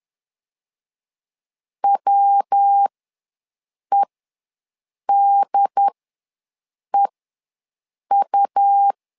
Morse code, heard as a keyed tone, about 10 wpm, spaced as for 6 wpm, 800 Hz. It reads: WEDEU
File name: LEZIONI / 3 MWRO / 3 MWR ws 10wpm 2151